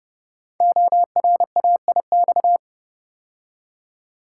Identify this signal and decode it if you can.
Morse code, heard as a keyed tone, about 30 wpm, 700 Hz. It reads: ORAIX
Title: Code Groups - 5 Letters - 30wpm